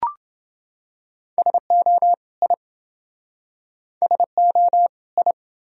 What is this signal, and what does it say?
Morse code, heard as a keyed tone, about 29 wpm, 700 Hz. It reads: SO I SO I